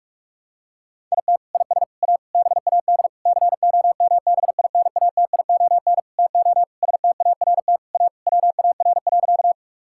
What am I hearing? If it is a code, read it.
Morse code, heard as a keyed tone, about 46 wpm, 700 Hz. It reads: IT IS A BAD COMBINATION TO START A WAR.
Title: Sentences from Top 1000 Words - Review Part 6 - 45wpm